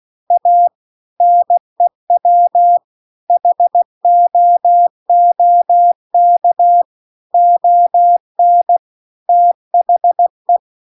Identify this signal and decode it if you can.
Morse code, heard as a keyed tone, about 16 wpm, 700 Hz. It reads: A NEW HOOK ON THE
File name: CW-The-Big-Twoheaded-River-2-16WPM